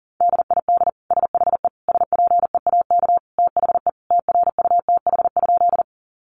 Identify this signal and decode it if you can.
Morse code, heard as a keyed tone, about 40 wpm, 700 Hz. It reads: DID SHE SPEAK THE TRUTH?
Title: Sentences from Top 1100 Words 40wpm